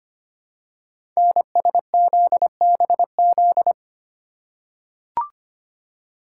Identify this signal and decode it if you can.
Morse code, heard as a keyed tone, about 25 wpm, 700 Hz. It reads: NSZBZ
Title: Code Groups - 5 Letters - 25wpm